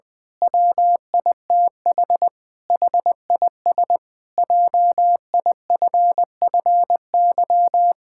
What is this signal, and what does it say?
Morse code, heard as a keyed tone, about 20 wpm, 700 Hz. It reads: WITH HIS JIFFY
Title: Pangram Challenge v2 - 20wpm